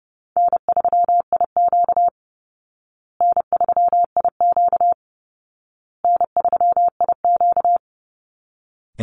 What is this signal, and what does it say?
Morse code, heard as a keyed tone, about 30 wpm, 700 Hz. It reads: N3IQ N3IQ N3IQ
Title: Easy Call Signs Repeated 3x - 30wpm